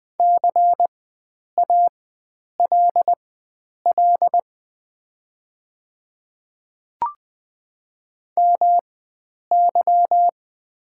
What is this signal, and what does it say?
Morse code, heard as a keyed tone, about 20 wpm, spaced as for 9 wpm, 700 Hz. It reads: CALL MY
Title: 4x Character Spacing - Top 100 Words - 20wpm